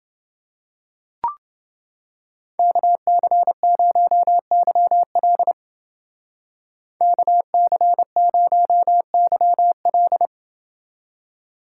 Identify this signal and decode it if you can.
Morse code, heard as a keyed tone, about 30 wpm, 700 Hz. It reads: KC0YL KC0YL